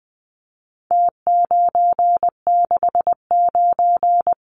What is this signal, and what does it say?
Morse code, heard as a keyed tone, about 20 wpm, 700 Hz. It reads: T969
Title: Numbers - Contest Style - 4-Digit 20wpm